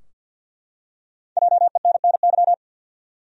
Morse code, heard as a keyed tone, about 52 wpm, 700 Hz. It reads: JENNY
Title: USA Common Names - 50wpm